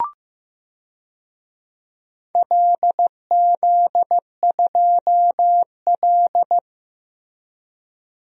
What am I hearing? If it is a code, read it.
Morse code, heard as a keyed tone, about 16 wpm, 700 Hz. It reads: LZ2L
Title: DXCC-100 Call Signs - 4 Characters - Repeated 3x - 15wpm